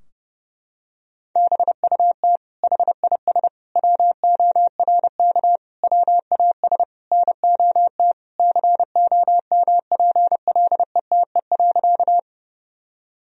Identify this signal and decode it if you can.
Morse code, heard as a keyed tone, about 30 wpm, 700 Hz. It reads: BUT HIS WORK WAS NOT COMPLETE.
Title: Sentences from Top 1000 Words - Review Part 2 - 30wpm